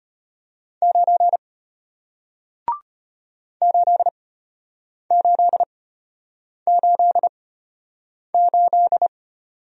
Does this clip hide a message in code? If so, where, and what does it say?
Yes: Morse code from 0.8 to 9.1 s, heard as a keyed tone, about 34 wpm, spaced as for 8 wpm, 700 Hz. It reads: 9 8888